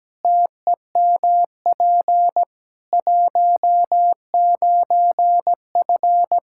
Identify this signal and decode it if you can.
Morse code, heard as a keyed tone, about 17 wpm, 700 Hz. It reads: TEMP 19F
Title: Practice QSO - Exchange 2 - Info - 17wpm